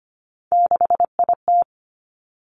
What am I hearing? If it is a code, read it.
Morse code, heard as a keyed tone, about 25 wpm, 700 Hz. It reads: 6IT
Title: Three Challenging Letter-Number Combinations 25wpm